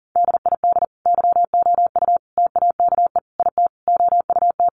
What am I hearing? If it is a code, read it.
Morse code, heard as a keyed tone, about 38 wpm, 700 Hz. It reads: DID YOU TAKE IT OUT
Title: Sentences from Top 1100 Words 40wpm